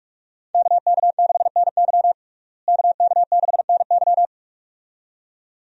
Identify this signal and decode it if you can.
Morse code, heard as a keyed tone, about 46 wpm, 700 Hz. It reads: KK6NY KK6NY
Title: USA Call Signs - 5 Characters - Repeated 2x - 45wpm